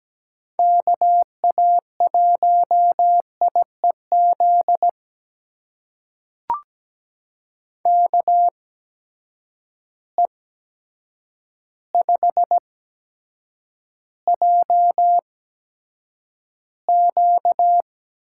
Morse code, heard as a keyed tone, about 17 wpm, spaced as for 8 wpm, 700 Hz. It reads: KA1IEZ KE5JQ
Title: USA Call Signs - 6 Characters - 8x Character Spacing - 17wpm